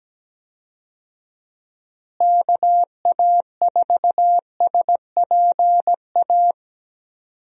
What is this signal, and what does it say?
Morse code, heard as a keyed tone, about 17 wpm, 700 Hz. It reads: KA4SPA
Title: USA Call Signs - 6 Characters - Repeated 2x - 17wpm